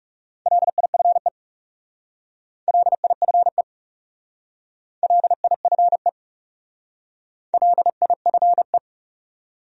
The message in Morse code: LIFE LIFE LIFE LIFE

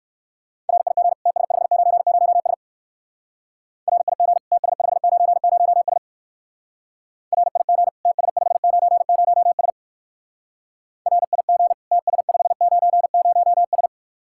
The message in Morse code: RIG TS590S RIG TS590S RIG TS590S RIG TS590S

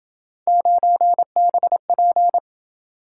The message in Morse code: 9BP

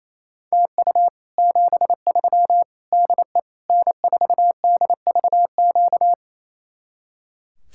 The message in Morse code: TU 73 DE N4DVQ

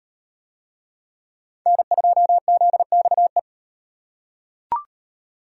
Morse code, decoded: N2ZXE

